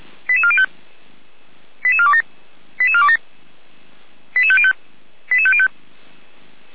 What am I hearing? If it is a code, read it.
Five-tone selective call (ZVEI), heard as quick runs of five tones, tones of about 70 ms. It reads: 88405 88427 88427 88505 88505